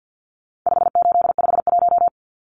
Morse code, heard as a keyed tone, about 50 wpm, 700 Hz. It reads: 5851
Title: Numbers - 4-Digit 50wpm